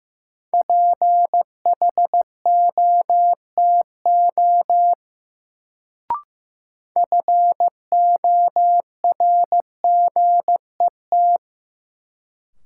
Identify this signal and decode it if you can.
Morse code, heard as a keyed tone, about 15 wpm, 700 Hz. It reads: PHOTO FORGET
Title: Sentences from Top 1200 Words - New Word Drill - 15wpm